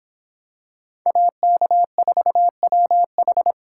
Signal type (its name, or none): Morse code